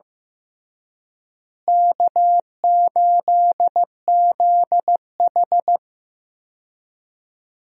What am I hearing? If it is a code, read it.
Morse code, heard as a keyed tone, about 15 wpm, 700 Hz. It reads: K8ZH
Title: USA Call Signs - 4 Characters - Repeated 3x - 15wpm